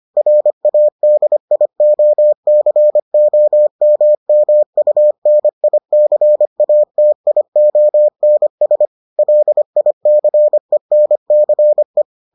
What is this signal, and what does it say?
Morse code, heard as a keyed tone, about 25 wpm, 600 Hz. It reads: RADIOCOMMUNICATIONS LICENCE